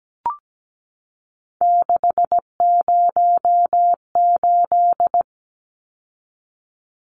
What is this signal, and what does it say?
Morse code, heard as a keyed tone, about 17 wpm, 700 Hz. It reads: 608
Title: Numbers - 3-Digit 17wpm